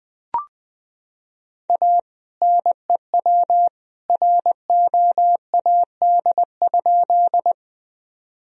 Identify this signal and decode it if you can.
Morse code, heard as a keyed tone, about 21 wpm, 700 Hz. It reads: A NEW ROAD?
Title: Sentences from Top 1000 Words - Review Part 2 - 20wpm